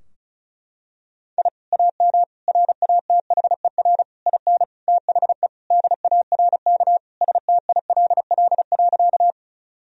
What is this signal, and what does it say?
Morse code, heard as a keyed tone, about 36 wpm, 700 Hz. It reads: I AM RATHER IN THE DARK STILL.
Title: Sentences from Top 1000 Words - Review Part 4 - 35wpm